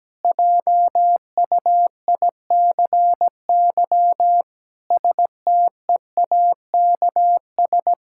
Morse code, heard as a keyed tone, about 17 wpm, 700 Hz. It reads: JUICY STEAKS
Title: Pangram Challenge v2 - 17wpm